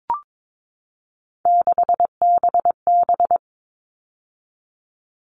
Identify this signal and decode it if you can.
Morse code, heard as a keyed tone, about 22 wpm, 700 Hz. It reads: 6BB
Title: All Cut Numbers - 3-Digit 22wpm